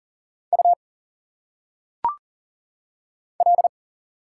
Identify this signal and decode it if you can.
Morse code, heard as a keyed tone, about 39 wpm, 700 Hz. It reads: U L
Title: Single Letters - Phonetic Pronunciation 40wpm